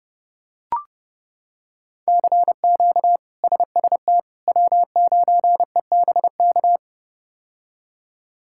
Morse code, heard as a keyed tone, about 32 wpm, 700 Hz. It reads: CQ SST W9EBK
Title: K1USN SST QSO Practice - 30wpm